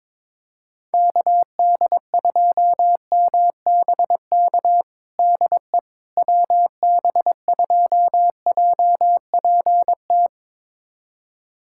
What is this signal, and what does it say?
Morse code, heard as a keyed tone, about 22 wpm, 700 Hz. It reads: KD2MBK DE WB2JPT
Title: USA Callsign DE Callsign 22wpm